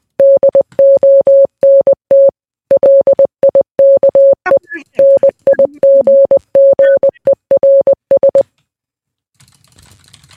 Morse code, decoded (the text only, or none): DONT LIKE NIGGERS